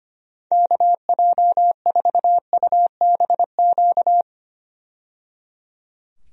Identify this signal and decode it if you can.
Morse code, heard as a keyed tone, about 25 wpm, 700 Hz. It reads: KJ4UBQ
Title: USA Call Signs - 6 Characters - 25wpm